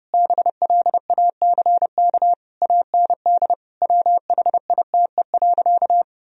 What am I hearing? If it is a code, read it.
Morse code, heard as a keyed tone, about 32 wpm, 700 Hz. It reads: BLACK AND WHITE.
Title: Sentences from Top 1000 Words - Review Part 2 - 30wpm